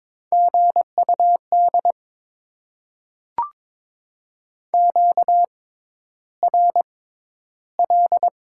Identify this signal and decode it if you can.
Morse code, heard as a keyed tone, about 22 wpm, spaced as for 11 wpm, 700 Hz. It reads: GUD QRL
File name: Top 20 QSO Elements - 6x Character Spacing - 22wpm